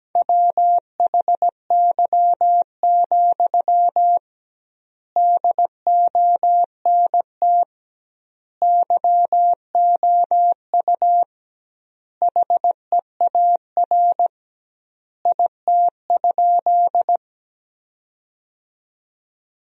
Morse code, heard as a keyed tone, about 17 wpm, 700 Hz. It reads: WHY, DONT YOU HEAR IT?
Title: 2x Word-Spacing - Sentences from Top 300 Words - 17wpm